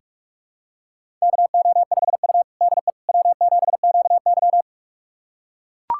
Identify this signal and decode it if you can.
Morse code, heard as a keyed tone, about 43 wpm, 700 Hz. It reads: KO5U DE W7QY
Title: USA Callsign DE Callsign - 4 Characters - 45wpm